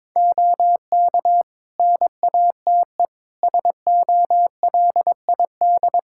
Morse code, heard as a keyed tone, about 21 wpm, 700 Hz. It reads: OK NATE SOLID
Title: Practice QSO - Exchange 3 - More Info - 22wpm